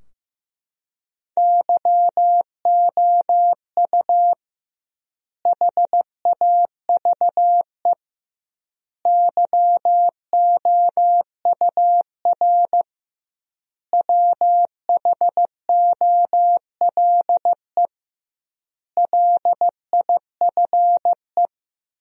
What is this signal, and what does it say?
Morse code, heard as a keyed tone, about 15 wpm, 700 Hz. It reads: YOU HAVE YOUR WHOLE LIFE
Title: 2x Word-Spacing - Sentences from Top 400 Words - 15wpm